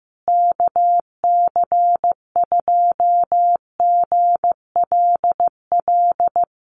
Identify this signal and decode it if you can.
Morse code, heard as a keyed tone, about 15 wpm, 700 Hz. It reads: KC2GLL
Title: USA Call Signs Repeated 3x 15wpm